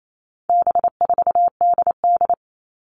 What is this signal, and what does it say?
Morse code, heard as a keyed tone, about 28 wpm, 700 Hz. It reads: B4DD
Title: All Cut Numbers - 4-Digit 28wpm